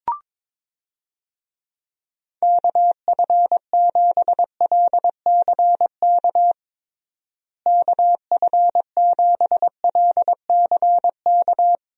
Morse code, heard as a keyed tone, about 22 wpm, 700 Hz. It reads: KF7LCK KF7LCK
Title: USA Call Signs - 6 Characters - Repeated 3x - 22wpm